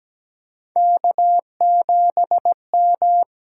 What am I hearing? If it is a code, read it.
Morse code, heard as a keyed tone, about 17 wpm, 700 Hz. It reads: K7M
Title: USA Call Signs - 4 Characters - 8x Character Spacing - 17wpm